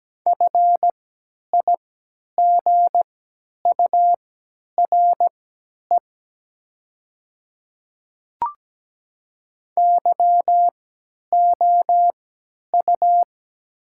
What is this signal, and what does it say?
Morse code, heard as a keyed tone, about 17 wpm, spaced as for 10 wpm, 700 Hz. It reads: FIGURE YOU
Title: 3x Character Spacing - Top 500 Words - 17wpm